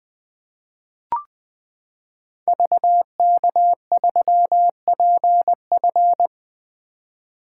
Morse code, heard as a keyed tone, about 20 wpm, 700 Hz. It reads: VK3PF
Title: 2023 SOTA Activator Call Signs - 20wpm